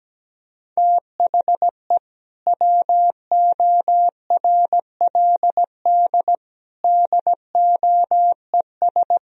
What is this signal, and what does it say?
Morse code, heard as a keyed tone, about 17 wpm, 700 Hz. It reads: THE WORLD DOES
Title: Sentences from Top 1000 Words - Review Part 1 - 17wpm